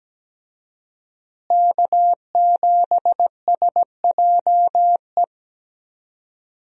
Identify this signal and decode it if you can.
Morse code, heard as a keyed tone, about 17 wpm, 700 Hz. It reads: K7SJE
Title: USA Call Signs - 5 Characters - Repeated 2x - 17wpm